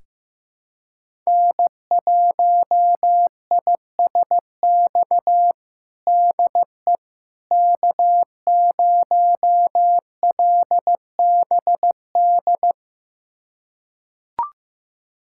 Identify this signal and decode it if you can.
Morse code, heard as a keyed tone, about 15 wpm, 700 Hz. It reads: N1ISX DE K0LBD
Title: USA Callsign DE Callsign - 5 Characters - 15wpm